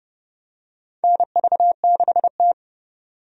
Morse code, heard as a keyed tone, about 32 wpm, 700 Hz. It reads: NV6T
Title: USA Call Signs - 4 Characters - 30wpm